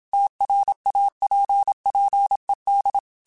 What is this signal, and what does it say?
Morse code, heard as a keyed tone, about 27 wpm, 800 Hz. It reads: TRAPPED
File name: QSOs / QSOs-mp3s / WA7QOX 2@25